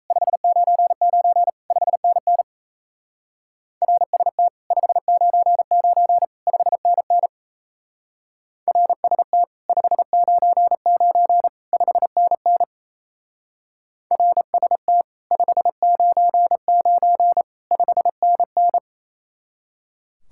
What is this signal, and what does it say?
Morse code, heard as a keyed tone, about 37 wpm, 700 Hz. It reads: 599 5NN RST 599 5NN RST 599 5NN RST 599 5NN